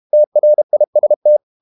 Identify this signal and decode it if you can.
Morse code, heard as a keyed tone, about 32 wpm, 600 Hz. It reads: TRIST
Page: Ferie 32 wpm